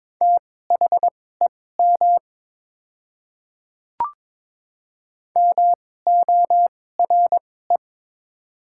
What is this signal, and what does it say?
Morse code, heard as a keyed tone, about 22 wpm, spaced as for 16 wpm, 690 Hz. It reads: THEM MORE
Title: 2x Character Spacing - Top 100 Words - 22wpm